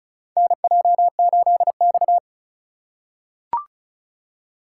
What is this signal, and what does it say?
Morse code, heard as a keyed tone, about 36 wpm, 700 Hz. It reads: NJ8X
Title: USA Call Signs - 4 Characters - 2x Character Spacing - 35wpm